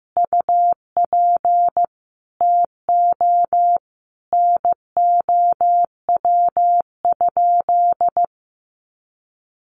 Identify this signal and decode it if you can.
Morse code, heard as a keyed tone, about 15 wpm, 700 Hz. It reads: UP TO NOW?
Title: Sentences from Top 100 Words 15wpm